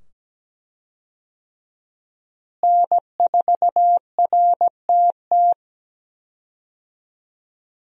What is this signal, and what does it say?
Morse code, heard as a keyed tone, about 17 wpm, 700 Hz. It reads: N4RTT
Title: USA Call Signs - 5 Characters - Repeated 2x - 17wpm